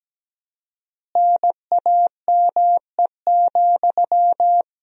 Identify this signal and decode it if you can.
Morse code, heard as a keyed tone, about 17 wpm, 700 Hz. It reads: NAME,
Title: 4x Word-Spacing - Sentences from Top 500 Words - 17wpm